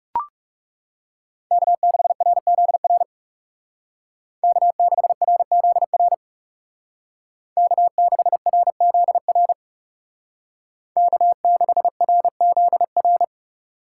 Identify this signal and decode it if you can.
Morse code, heard as a keyed tone, about 37 wpm, 700 Hz. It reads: K6RZR K6RZR K6RZR K6RZR
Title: Speed Racing - USA 1x3 Call Signs 30wpm